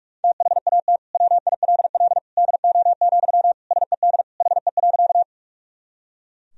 Morse code, heard as a keyed tone, about 46 wpm, 700 Hz. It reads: THAT WILL DO, SED HE.
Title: Sentences from Top 100 Words and Abbreviations 45wpm